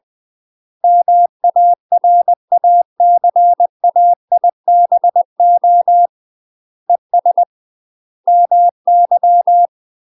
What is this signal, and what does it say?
Morse code, heard as a keyed tone, about 20 wpm, 700 Hz. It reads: MARACAIBO ES MY
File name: UK MASTER QSOs TRAINING / 20WPM / UK MASTER QSOs 0072